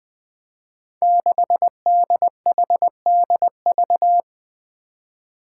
Morse code, heard as a keyed tone, about 20 wpm, 700 Hz. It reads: 6DHDV
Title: Five Challenging Letter-Number Combinations 20wpm